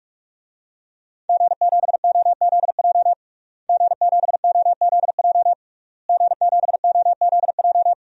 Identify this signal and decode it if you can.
Morse code, heard as a keyed tone, about 41 wpm, 700 Hz. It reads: G7OZJ G7OZJ G7OZJ
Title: DXCC-100 Call Signs - 5 Characters - Repeated 3x - 45wpm